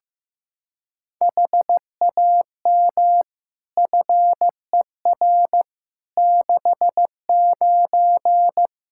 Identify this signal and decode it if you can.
Morse code, heard as a keyed tone, about 15 wpm, 700 Hz. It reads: HAM FER 69
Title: K1USN SST QSO Practice 15wpm